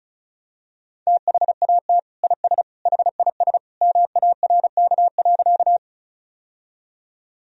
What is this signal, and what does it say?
Morse code, heard as a keyed tone, about 36 wpm, 700 Hz. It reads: THAT IS HIS MARK.
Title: Sentences from Top 1000 Words - Review Part 1 - 35wpm